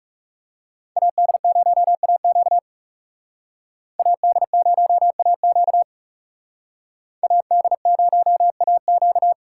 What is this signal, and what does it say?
Morse code, heard as a keyed tone, about 36 wpm, 700 Hz. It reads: AD0AQ AD0AQ AD0AQ
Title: Speed Racing - USA 2x2 Call Signs 30wpm